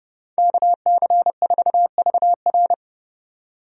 Morse code, heard as a keyed tone, about 30 wpm, 700 Hz. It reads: KC4VR
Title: USA Call Signs - 5 Characters - 30wpm